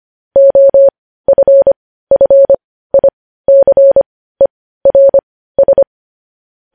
Morse code, heard as a keyed tone, about 25 wpm, spaced as for 15 wpm, 550 Hz. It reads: OFFICERS
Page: news-headlines-in-morse-code-at-15-wpm